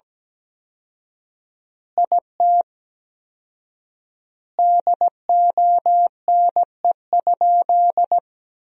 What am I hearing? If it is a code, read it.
Morse code, heard as a keyed tone, about 17 wpm, 700 Hz. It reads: IT DONE?
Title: 4x Word-Spacing - Sentences from Top 500 Words - 17wpm